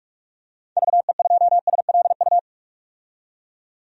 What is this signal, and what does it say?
Morse code, heard as a keyed tone, about 46 wpm, 700 Hz. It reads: VE2SLU
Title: DXCC-100 Call Signs - 6 Characters - 45wpm